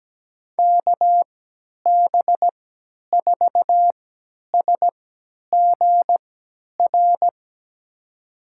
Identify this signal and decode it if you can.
Morse code, heard as a keyed tone, about 17 wpm, spaced as for 10 wpm, 700 Hz. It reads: KB4SGR
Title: USA Call Signs - 3x Character Spacing - 17wpm